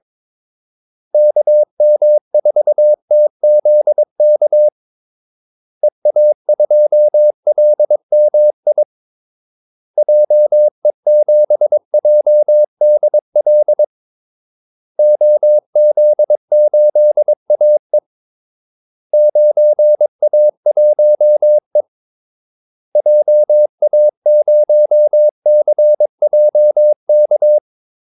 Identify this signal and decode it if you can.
Morse code, heard as a keyed tone, about 22 wpm, 600 Hz. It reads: KM4TZK EA2LMI JE7JDL OZ8AE 9A1E JA0CJK
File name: Callsigns-22wpm